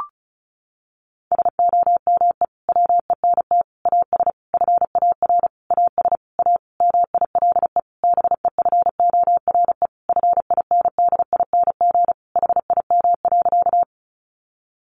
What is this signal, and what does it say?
Morse code, heard as a keyed tone, about 34 wpm, 700 Hz. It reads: SOME WENT AS FAR AS A MILE BEFORE FINDING HIM.
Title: Sentences from Top 1100 Words 35wpm